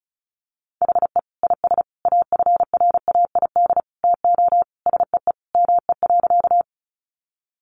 Morse code, heard as a keyed tone, about 34 wpm, 700 Hz. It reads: HE IS AFRAID TO SEE ME.
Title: Sentences from Top 1500 Words 35wpm